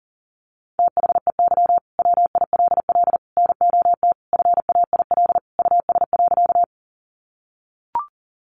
Morse code, heard as a keyed tone, about 40 wpm, 700 Hz. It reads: THEY WILL NOT FAIL US.
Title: Sentences from Top 1100 Words 40wpm